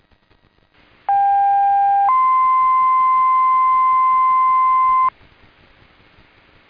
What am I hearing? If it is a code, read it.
Two-tone page, heard as a one-second tone, then a three-second tone, about 800 then 1040 Hz. It is heard from 1.1 to 5.1 s.
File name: Blue Mountain EMS 2026 07 28 19 01 03